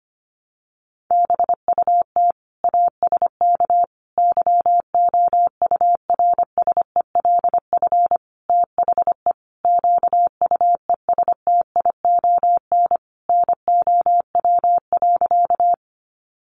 Morse code, heard as a keyed tone, about 25 wpm, 700 Hz. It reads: BUT ASK YOURSELF THE QUESTION NOW.
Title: Sentences from Top 600 Words 25wpm